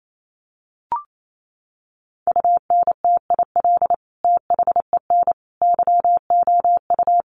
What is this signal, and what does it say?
Morse code, heard as a keyed tone, about 28 wpm, 700 Hz. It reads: UNTIL THEN YOU
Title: Sentences from Top 600 Words 28wpm